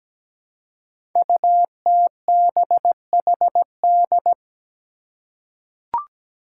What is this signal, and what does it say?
Morse code, heard as a keyed tone, about 17 wpm, 700 Hz. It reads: UTBHD
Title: Five Challenging Letter-Number Combinations 17wpm